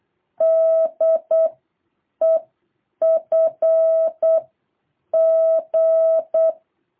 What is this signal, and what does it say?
Morse code, heard as a keyed tone, about 8 wpm, 660 Hz. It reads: D E F G
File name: jh8cbh / 2023cw